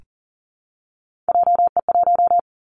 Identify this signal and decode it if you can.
Morse code, heard as a keyed tone, about 40 wpm, 700 Hz. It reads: JE1
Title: Callsign Prefixes for Top 25 DXCC - Repeated 3x - 40wpm